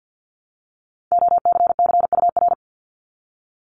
Morse code, heard as a keyed tone, about 50 wpm, 700 Hz. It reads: OCCUR
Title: Multi-letter Phonemes 50wpm